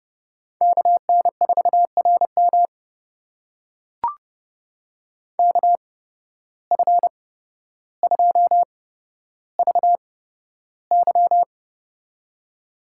Morse code, heard as a keyed tone, about 30 wpm, spaced as for 13 wpm, 700 Hz. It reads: KN4RM KF2VY
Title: USA Call Signs - 8x Character Spacing - 30wpm